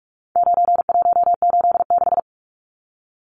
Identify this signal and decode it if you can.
Morse code, heard as a keyed tone, about 44 wpm, 700 Hz. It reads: 9186
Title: Numbers - 4-Digit 45wpm